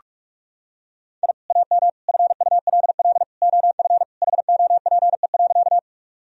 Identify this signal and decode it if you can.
Morse code, heard as a keyed tone, about 43 wpm, 700 Hz. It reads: I AM FULL OF HOPE.
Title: Sentences from Top 1000 Words - Review Part 3 - 45wpm